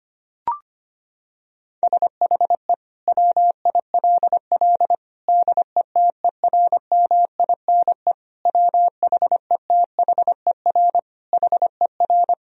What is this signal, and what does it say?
Morse code, heard as a keyed tone, about 25 wpm, 700 Hz. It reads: SHE WILL DETERMINE WHETHER HER
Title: Sentences from Top 1000 Words - Review Part 4 - 25wpm